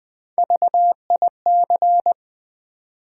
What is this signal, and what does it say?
Morse code, heard as a keyed tone, about 20 wpm, 700 Hz. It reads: VIC
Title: Build A Word - 3 Letters At A Time - 6-Letter Words - 20wpm